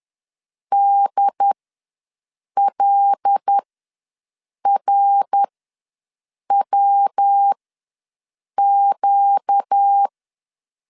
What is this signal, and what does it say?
Morse code, heard as a keyed tone, about 10 wpm, spaced as for 6 wpm, 800 Hz. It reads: DLRWQ
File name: LEZIONI / 5 LBVC / 5 L ws 10wpm 3636